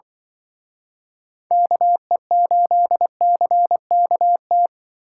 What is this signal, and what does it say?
Morse code, heard as a keyed tone, about 24 wpm, 700 Hz. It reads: KE8CKT
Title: Speed Racing - USA Call Signs 20wpm